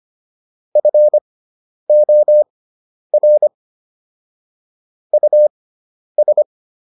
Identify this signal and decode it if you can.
Morse code, heard as a keyed tone, about 25 wpm, spaced as for 10 wpm, 600 Hz. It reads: FOR US